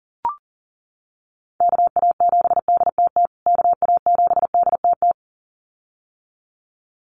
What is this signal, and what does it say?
Morse code, heard as a keyed tone, about 40 wpm, 700 Hz. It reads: KA7DTT KA7DTT
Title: USA Call Signs Repeated 2x 40wpm